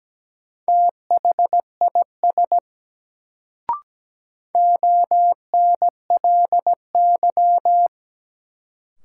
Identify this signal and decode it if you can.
Morse code, heard as a keyed tone, about 17 wpm, 700 Hz. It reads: THIS ONLY
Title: Sets of 2 Words - New Word Drill - 17wpm